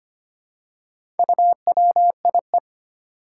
Morse code, heard as a keyed tone, about 25 wpm, 700 Hz. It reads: UWIE
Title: Four Challenging Letter-Number Combinations 25wpm